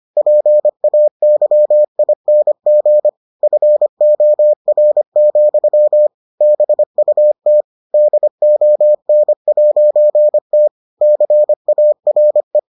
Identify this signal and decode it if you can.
Morse code, heard as a keyed tone, about 25 wpm, 600 Hz. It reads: PAYING FOR, BUT DON'T CARE